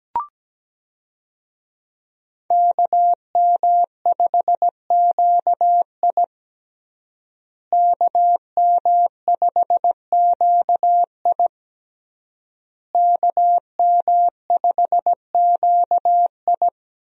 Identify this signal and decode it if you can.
Morse code, heard as a keyed tone, about 17 wpm, 700 Hz. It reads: KM5QI KM5QI KM5QI
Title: USA Call Signs - 5 Characters - Repeated 3x - 17wpm